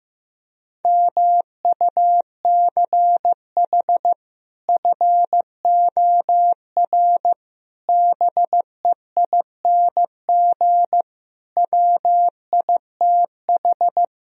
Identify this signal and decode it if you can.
Morse code, heard as a keyed tone, about 15 wpm, 700 Hz. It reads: MUCH FOR BEING WITH
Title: Speed-Racing - Sets of 5 Words 15wpm